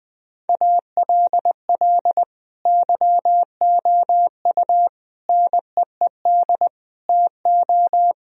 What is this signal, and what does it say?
Morse code, heard as a keyed tone, about 19 wpm, 700 Hz. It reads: ALL YOU NEED TO
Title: Sentences from Top 1000 Words - Review Part 4 - 20wpm